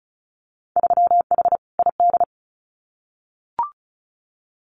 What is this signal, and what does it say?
Morse code, heard as a keyed tone, about 36 wpm, 700 Hz. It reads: 3H ID
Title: Winter Field Day Exchanges Encore 35wpm